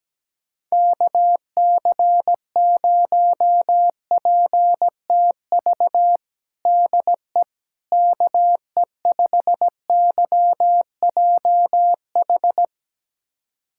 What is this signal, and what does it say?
Morse code, heard as a keyed tone, about 17 wpm, 700 Hz. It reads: KC0PTV DE KE5YJH
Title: USA Callsign DE Callsign 17wpm